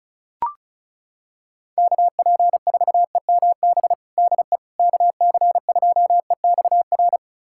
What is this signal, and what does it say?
Morse code, heard as a keyed tone, about 36 wpm, 700 Hz. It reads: KP4EMB DE KC2EXR
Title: USA Callsign DE Callsign 35wpm